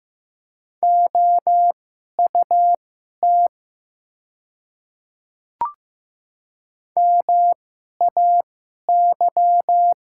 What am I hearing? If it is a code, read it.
Morse code, heard as a keyed tone, about 15 wpm, spaced as for 11 wpm, 700 Hz. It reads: OUT MAY
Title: 2x Character Spacing - Top 100 Words - 15wpm